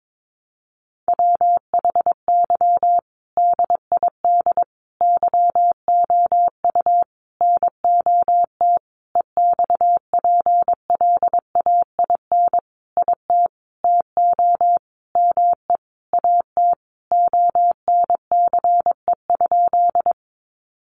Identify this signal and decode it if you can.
Morse code, heard as a keyed tone, about 22 wpm, 700 Hz. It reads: WHY DID YOU NOT EXPLAIN IT TO ME AT ONCE?
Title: Sentences from Top 1100 Words 22wpm